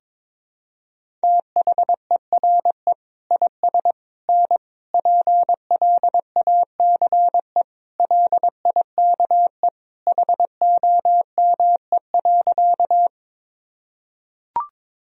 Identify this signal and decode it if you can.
Morse code, heard as a keyed tone, about 22 wpm, 700 Hz. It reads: THERE IS N PLACE LIKE HOME.
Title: Sentences from Top 200 Words and Abbreviations 22wpm